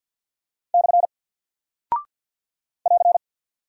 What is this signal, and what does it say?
Morse code, heard as a keyed tone, about 50 wpm, 700 Hz. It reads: / <AR>
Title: Single Character (Letter:Number:Punctuation:Prosign) 50wpm